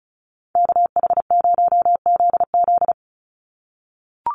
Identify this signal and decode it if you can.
Morse code, heard as a keyed tone, about 34 wpm, 700 Hz. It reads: KH0ZZ
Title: Call Signs - 35wpm